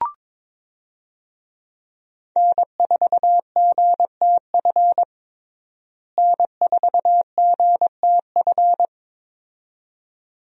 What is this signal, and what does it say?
Morse code, heard as a keyed tone, about 22 wpm, 700 Hz. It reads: N4GTF N4GTF
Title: USA Call Signs - 5 Characters - Repeated 2x - 22wpm